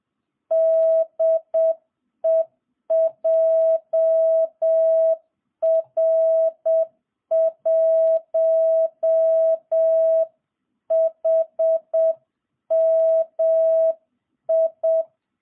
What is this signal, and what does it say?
Morse code, heard as a keyed tone, about 7 wpm, 660 Hz. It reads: DE JR1HMI